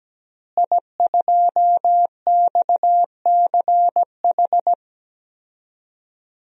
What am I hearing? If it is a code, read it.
Morse code, heard as a keyed tone, about 17 wpm, 700 Hz. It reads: I2XCH
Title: DXCC-100 Call Signs - 5 Characters - Repeated 3x - 17wpm